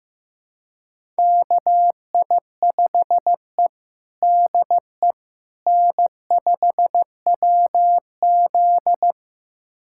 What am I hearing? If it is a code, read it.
Morse code, heard as a keyed tone, about 15 wpm, 700 Hz. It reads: KI5E DE N5WZ
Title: USA Callsign DE Callsign - 4 Characters - 15wpm